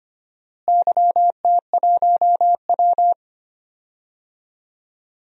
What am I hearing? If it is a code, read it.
Morse code, heard as a keyed tone, about 25 wpm, 700 Hz. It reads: YT1W
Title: DXCC-100 Call Signs - 4 Characters - Repeated 2x - 25wpm